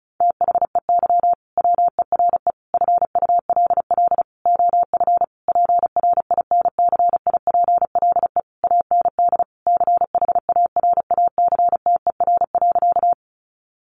Morse code, heard as a keyed tone, about 34 wpm, 700 Hz. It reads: THEY WERE FULL OF PRINCIPLE AND CHARACTER.
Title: Sentences from Top 1400 Words 35wpm